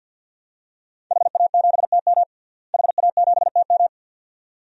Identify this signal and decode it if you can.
Morse code, heard as a keyed tone, about 52 wpm, 700 Hz. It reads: HA7TM HA7TM
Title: DXCC-100 Call Signs - 5 Characters - Repeated 2x - 50wpm